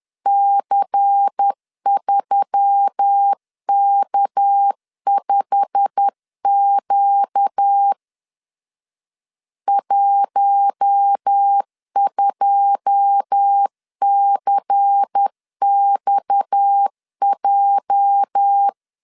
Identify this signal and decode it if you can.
Morse code, heard as a keyed tone, about 10 wpm, 800 Hz. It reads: C3K5Q 12CXJ
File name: LEZIONI / 9 2 10wpm 2736